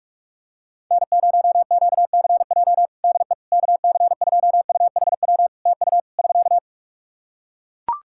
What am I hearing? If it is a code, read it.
Morse code, heard as a keyed tone, about 43 wpm, 700 Hz. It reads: N0QCJ DE KC2UHW TU <SK>
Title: K1USN SST QSO Practice 45wpm